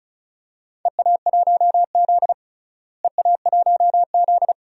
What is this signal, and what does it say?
Morse code, heard as a keyed tone, about 36 wpm, 700 Hz. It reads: EA1Z EA1Z
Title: DXCC-100 Call Signs - 4 Characters - Repeated 3x - 35wpm